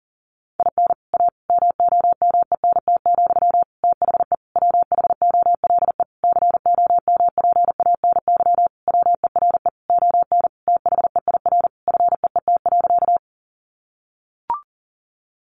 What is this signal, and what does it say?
Morse code, heard as a keyed tone, about 40 wpm, 700 Hz. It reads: IN A MOMENT, THE WHOLE COMPANY WERE ON THEIR FEET.
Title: Sentences from Top 1000 Words 40wpm